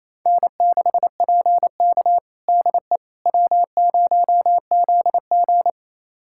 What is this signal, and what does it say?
Morse code, heard as a keyed tone, about 28 wpm, 700 Hz. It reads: N6PK DE W0ZG
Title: USA Callsign DE Callsign - 4 Characters - 28wpm